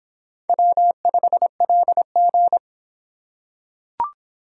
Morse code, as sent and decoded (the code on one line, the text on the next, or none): .-- ..... .-.. --.
W5LG